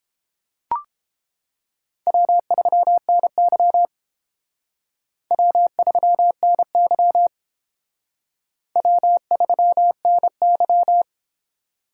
Morse code, heard as a keyed tone, about 28 wpm, 700 Hz. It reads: W3NY W3NY W3NY